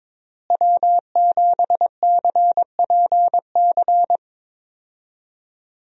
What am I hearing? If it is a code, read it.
Morse code, heard as a keyed tone, about 21 wpm, 700 Hz. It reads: W7CPC
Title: USA Call Signs - 5 Characters - Repeated 2x - 22wpm